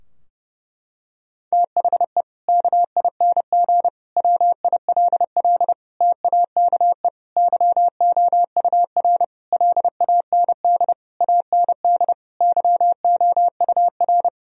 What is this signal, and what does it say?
Morse code, heard as a keyed tone, about 30 wpm, 700 Hz. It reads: THE KING WILL TAKE YOUR LAND AND YOUR
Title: Sentences from Top 1000 Words - Review Part 4 - 30wpm